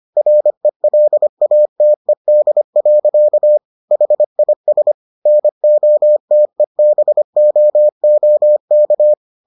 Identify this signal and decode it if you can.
Morse code, heard as a keyed tone, about 25 wpm, 600 Hz. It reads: RELATED. HIS NOTEBOOK